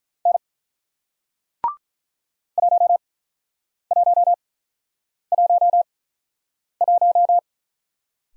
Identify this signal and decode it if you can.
Morse code, heard as a keyed tone, about 48 wpm, spaced as for 9 wpm, 700 Hz. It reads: N 1111